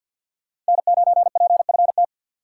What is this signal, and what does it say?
Morse code, heard as a keyed tone, about 53 wpm, 700 Hz. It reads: N9PFT
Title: USA Call Signs - 5 Characters - 6x Character Spacing - 50wpm